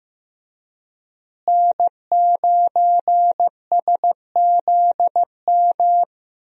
Morse code, heard as a keyed tone, about 15 wpm, 700 Hz. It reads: N9SZM
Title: USA Call Signs - 5 Characters - Repeated 3x - 15wpm